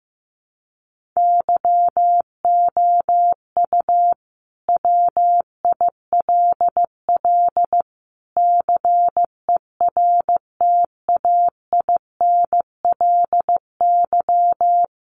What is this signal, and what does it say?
Morse code, heard as a keyed tone, about 15 wpm, 700 Hz. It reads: YOU WILL CERTAINLY
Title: Sentences from Top 1400 Words 15wpm